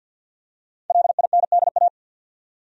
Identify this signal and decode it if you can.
Morse code, heard as a keyed tone, about 52 wpm, 700 Hz. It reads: LINDA
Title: USA Common Names - Max 5 Letters - 50wpm